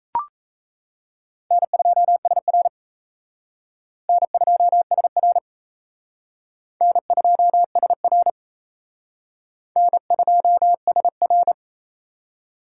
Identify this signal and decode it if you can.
Morse code, heard as a keyed tone, about 37 wpm, 700 Hz. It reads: N2SR N2SR N2SR N2SR